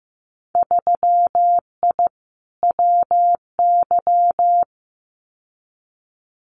Morse code, heard as a keyed tone, about 15 wpm, 700 Hz. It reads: 3I WY
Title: Winter Field Day Exchanges Encore 15wpm